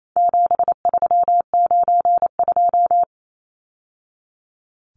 Morse code, heard as a keyed tone, about 28 wpm, 700 Hz. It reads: 7392
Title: Numbers - 4-Digit 28wpm